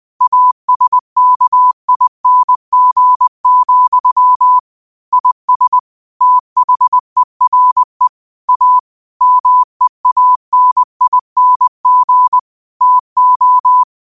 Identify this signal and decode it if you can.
Morse code, heard as a keyed tone, about 20 wpm, 1000 Hz. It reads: ASKING, IS THERE A MEANING TO
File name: qotdfiles / 20wpm